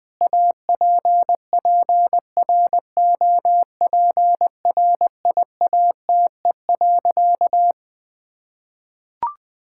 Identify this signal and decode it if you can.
Morse code, heard as a keyed tone, about 20 wpm, 700 Hz. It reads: APPROPRIATE.
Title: Sentences from Top 1000 Words - Review Part 5 - 20wpm